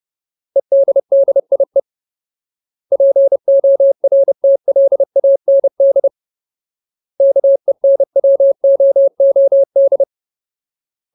Morse code, heard as a keyed tone, about 30 wpm, 550 Hz. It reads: EDDIE PORTLAND KENWOOD
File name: Ultimate Word List / 30WPM Ultimate